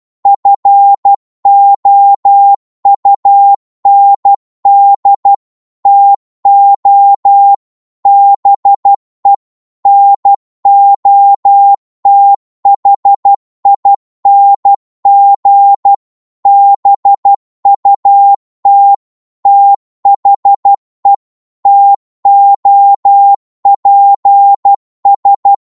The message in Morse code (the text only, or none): FOUND TO BE NOTHING BUT THE TOPS